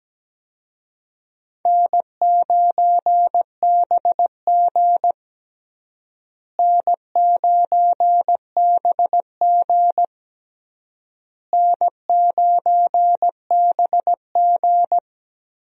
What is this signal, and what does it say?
Morse code, heard as a keyed tone, about 17 wpm, 700 Hz. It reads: N9BG N9BG N9BG
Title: USA Call Signs - 4 Characters - Repeated 3x - 17wpm